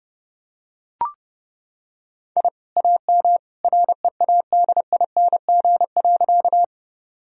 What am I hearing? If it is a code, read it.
Morse code, heard as a keyed tone, about 30 wpm, 700 Hz. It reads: I AM READING.
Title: Sentences from Top 1000 Words - Review Part 3 - 30wpm